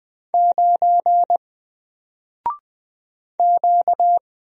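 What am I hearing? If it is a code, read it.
Morse code, heard as a keyed tone, about 20 wpm, 700 Hz. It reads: 9 Q
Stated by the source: Single letter and number Morse drill at 20 wpm, from N4LCD.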